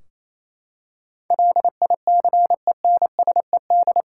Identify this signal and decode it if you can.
Morse code, heard as a keyed tone, about 28 wpm, 700 Hz. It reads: LICENSED
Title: Practice QSO - Exchange 3 - More Info - 28wpm